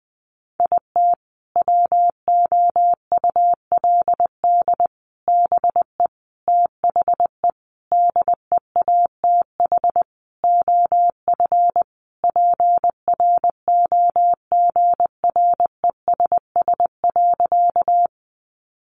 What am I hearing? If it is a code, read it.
Morse code, heard as a keyed tone, about 20 wpm, 700 Hz. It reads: IT WOULD BE THE DEATH OF PROGRESS.
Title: Sentences from Top 1400 Words 20wpm